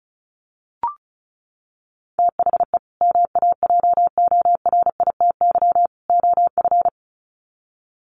Morse code, heard as a keyed tone, about 34 wpm, 700 Hz. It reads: THE MAJORITY OF
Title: Sets of 3 Words 35wpm